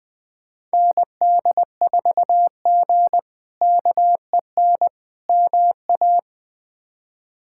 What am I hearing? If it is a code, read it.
Morse code, heard as a keyed tone, about 20 wpm, 700 Hz. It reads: ND4G KEN MA
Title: K1USN SST QSO Practice - 20wpm